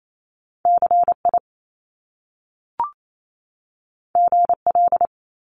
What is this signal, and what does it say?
Morse code, heard as a keyed tone, about 28 wpm, 700 Hz. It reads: CI GL
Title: Top Two Letter Combinations in Words 28wpm